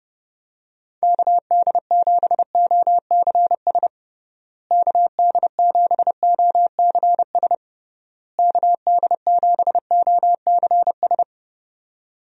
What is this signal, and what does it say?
Morse code, heard as a keyed tone, about 28 wpm, 700 Hz. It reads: KD7OCS KD7OCS KD7OCS
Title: USA Call Signs - 6 Characters - Repeated 3x - 30wpm